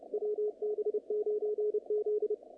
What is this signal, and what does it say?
Morse code, heard as a keyed tone, about 30 wpm, 440 Hz, with radio noise. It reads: WB9Z
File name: contests / naqp-cw-k5tr-op-k5pi- -2200